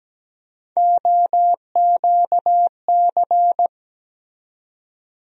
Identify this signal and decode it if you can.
Morse code, heard as a keyed tone, about 17 wpm, 700 Hz. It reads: OQC
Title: USA Call Sign Postfixes - 3 Letters 17wpm